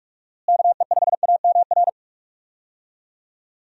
Morse code, heard as a keyed tone, about 48 wpm, 700 Hz. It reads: KE5AMR
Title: USA Call Signs - 6 Characters - Repeated 3x - 45wpm